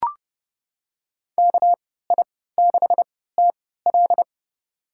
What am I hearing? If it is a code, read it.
Morse code, heard as a keyed tone, about 30 wpm, spaced as for 17 wpm, 700 Hz. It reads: KI6TL
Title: USA Call Signs - 5 Characters - 3x Character Spacing - 30wpm